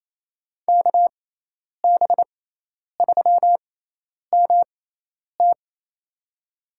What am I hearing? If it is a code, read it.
Morse code, heard as a keyed tone, about 28 wpm, spaced as for 10 wpm, 700 Hz. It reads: KB3MT
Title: USA Call Signs - 6x Character Spacing - 28wpm